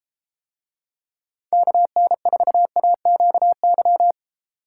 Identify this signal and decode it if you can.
Morse code, heard as a keyed tone, about 33 wpm, 700 Hz. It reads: KN4AQY